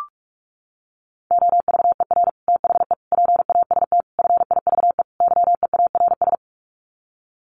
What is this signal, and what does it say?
Morse code, heard as a keyed tone, about 44 wpm, 700 Hz. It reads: OVER THE PAST FIVE YEARS
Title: Sets of 5 Words Encore 45wpm